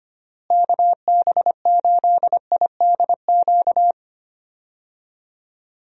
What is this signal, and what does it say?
Morse code, heard as a keyed tone, about 25 wpm, 700 Hz. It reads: KB8IDQ
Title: USA Call Signs - 6 Characters - Repeated 3x - 25wpm